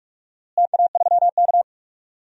Morse code, heard as a keyed tone, about 46 wpm, 700 Hz. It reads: TA3K
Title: DXCC-100 Call Signs - 4 Characters - Repeated 3x - 45wpm